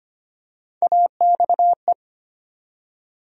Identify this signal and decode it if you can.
Morse code, heard as a keyed tone, about 25 wpm, 700 Hz. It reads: AXE
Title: USA Call Sign Postfixes - 3 Letters 25wpm